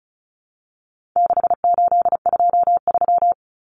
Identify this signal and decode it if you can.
Morse code, heard as a keyed tone, about 34 wpm, 700 Hz. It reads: 6823
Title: Numbers - 4-Digit 35wpm